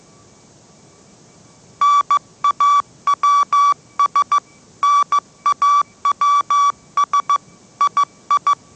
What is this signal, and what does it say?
Morse code, heard as a keyed tone, about 15 wpm, 1150 Hz. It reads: NAWS NAWS II
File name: mrx / mrx065 16wpm